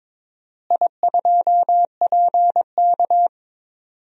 Morse code, heard as a keyed tone, about 22 wpm, 700 Hz. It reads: I2PK
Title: DXCC-100 Call Signs - 4 Characters - Repeated 3x - 22wpm